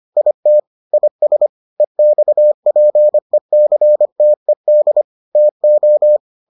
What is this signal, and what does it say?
Morse code, heard as a keyed tone, about 25 wpm, 600 Hz. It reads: IT IS EXPECTED TO